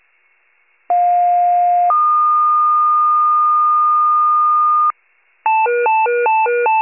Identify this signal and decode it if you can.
Two-tone page, heard as a one-second tone, then a three-second tone, about 710 then 1220 Hz. It is heard from 0.9 to 4.9 s.
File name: wcalert